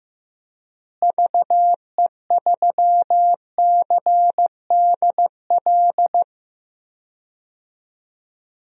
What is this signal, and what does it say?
Morse code, heard as a keyed tone, about 15 wpm, 700 Hz. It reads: VE3CDL
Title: DXCC-100 Call Signs - 6 Characters - Repeated 2x - 15wpm